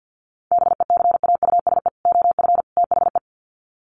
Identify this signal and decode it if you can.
Morse code, heard as a keyed tone, about 50 wpm, 700 Hz. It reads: BECAUSE OF THE
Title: Sets of 3 Words 50wpm